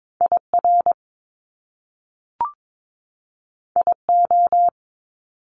Morse code, heard as a keyed tone, about 22 wpm, 700 Hz. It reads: IR IO
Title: Top Two Letter Combinations in Words 22wpm